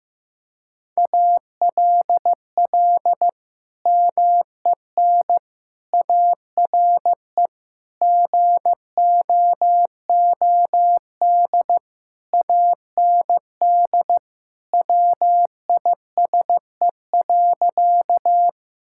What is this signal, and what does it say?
Morse code, heard as a keyed tone, about 15 wpm, 700 Hz. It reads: ALL MEN ARE GOOD AND WISE.
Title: Sentences from Top 1600 Words 15wpm